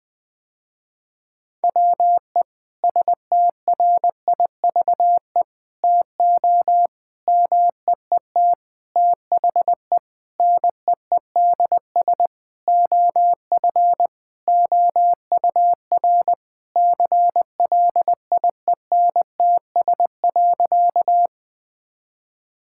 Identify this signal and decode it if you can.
Morse code, heard as a keyed tone, about 21 wpm, 700 Hz. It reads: WE STRIVE TO MEET THE NEEDS OF OUR CLIENTS.
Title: Sentences with 5-Word Phrases - Spoken Beginning - 20wpm